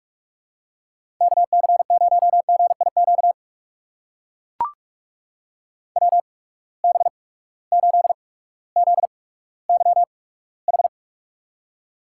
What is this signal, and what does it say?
Morse code, heard as a keyed tone, about 42 wpm, spaced as for 19 wpm, 700 Hz. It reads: KC0GIQ WB8ZYH
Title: USA Call Signs - 8x Character Spacing - 45wpm